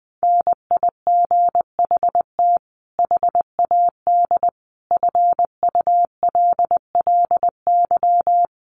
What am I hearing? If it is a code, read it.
Morse code, heard as a keyed tone, about 20 wpm, 700 Hz. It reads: NIGHT HAD FULLY
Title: Sentences from Top 1300 Words 20wpm